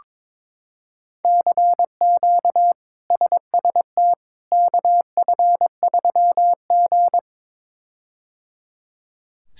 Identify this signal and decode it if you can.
Morse code, heard as a keyed tone, about 22 wpm, 700 Hz. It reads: CQ SST KF3G
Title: K1USN SST QSO Practice - 22wpm